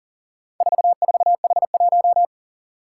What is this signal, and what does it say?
Morse code, heard as a keyed tone, about 40 wpm, 700 Hz. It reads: 44H1
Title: Four Challenging Letter-Number Combinations 40wpm